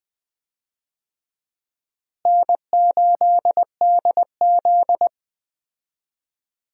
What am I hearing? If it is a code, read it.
Morse code, heard as a keyed tone, about 20 wpm, 700 Hz. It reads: N8DZ
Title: USA Call Signs - 4 Characters - Repeated 2x - 20wpm